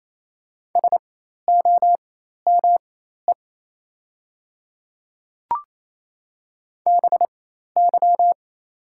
Morse code, heard as a keyed tone, about 28 wpm, spaced as for 13 wpm, 700 Hz. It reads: SOME BY
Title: 4x Character Spacing - Top 100 Words - 28wpm